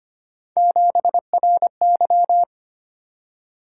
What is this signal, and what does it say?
Morse code, heard as a keyed tone, about 29 wpm, 700 Hz. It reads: 7RY
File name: Speed-Racing - Three Letter-Number Combinations - 25wpm